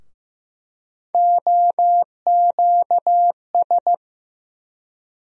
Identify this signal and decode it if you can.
Morse code, heard as a keyed tone, about 15 wpm, 700 Hz. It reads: OQS
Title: USA Call Sign Postfixes - 3 Letters 15wpm